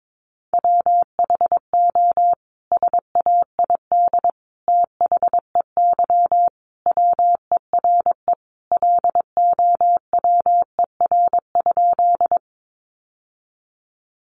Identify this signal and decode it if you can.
Morse code, heard as a keyed tone, about 22 wpm, 700 Hz. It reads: WHO SAID THEY WERE LOWER?
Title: Sentences from Top 700 Words 22wpm